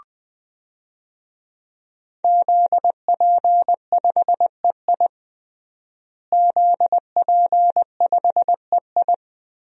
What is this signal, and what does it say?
Morse code, heard as a keyed tone, about 20 wpm, 700 Hz. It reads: ZP5EI ZP5EI
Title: DXCC-100 Call Signs - 5 Characters - Repeated 2x - 20wpm